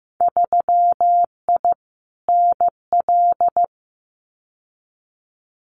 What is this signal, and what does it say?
Morse code, heard as a keyed tone, about 15 wpm, 700 Hz. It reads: 3I NL
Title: Winter Field Day Exchanges 15wpm